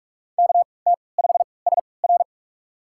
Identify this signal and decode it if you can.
Morse code, heard as a keyed tone, about 44 wpm, spaced as for 25 wpm, 700 Hz. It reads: KT5SR
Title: USA Call Signs - 5 Characters - 3x Character Spacing - 45wpm